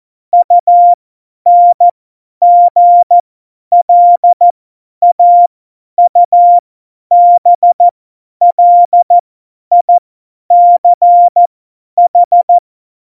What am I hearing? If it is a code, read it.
Morse code, heard as a keyed tone, about 14 wpm, spaced as for 10 wpm, 700 Hz. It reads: UNGLAUBLICH
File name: Morsekurs / Witze / Witz 05 Mai 2020, 50 F70